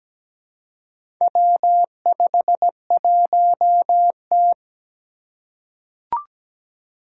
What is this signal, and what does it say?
Morse code, heard as a keyed tone, about 17 wpm, 700 Hz. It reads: W51T
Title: Four Challenging Letter-Number Combinations 17wpm